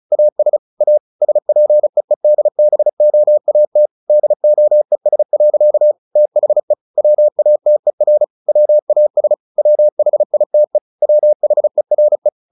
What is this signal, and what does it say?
Morse code, heard as a keyed tone, about 34 wpm, 600 Hz. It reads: AS A SPEEDBOAT DOES. THE WATER WAS WHITE WHERE
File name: the old man and the sea 2 35wpm